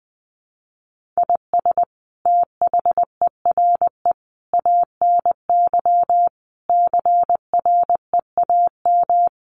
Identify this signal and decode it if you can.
Morse code, heard as a keyed tone, about 20 wpm, 700 Hz. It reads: IS THERE ANY CREAM
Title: Sentences from Top 1500 Words 20wpm